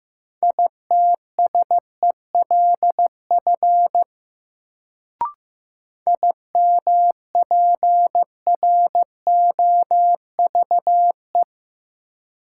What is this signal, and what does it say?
Morse code, heard as a keyed tone, about 15 wpm, 700 Hz. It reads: ITSELF IMPROVE
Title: Sentences from Top 800 Words - New Word Drill - 15wpm